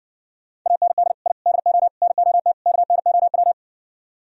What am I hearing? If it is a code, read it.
Morse code, heard as a keyed tone, about 60 wpm, 700 Hz. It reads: AND I DO NOT KNOW